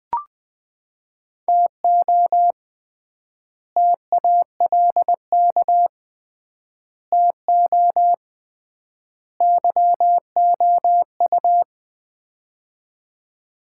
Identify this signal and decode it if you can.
Morse code, heard as a keyed tone, about 20 wpm, 700 Hz. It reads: TO TALK TO YOU
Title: 3x Word-Spacing - Sets of 4 Words - 20wpm